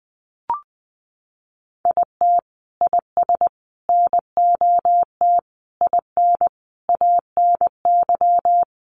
Morse code, heard as a keyed tone, about 20 wpm, 700 Hz. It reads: IT IS NOT IN ANY
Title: Sentences from Top 1200 Words 20wpm